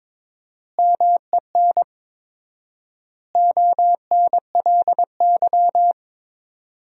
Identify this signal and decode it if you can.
Morse code, heard as a keyed tone, about 22 wpm, 700 Hz. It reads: MEN ONLY
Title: 4x Word-Spacing - Sentences from Top 500 Words - 22wpm